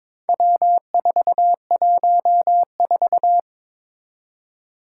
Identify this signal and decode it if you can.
Morse code, heard as a keyed tone, about 22 wpm, 700 Hz. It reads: W414
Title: Four Challenging Letter-Number Combinations 22wpm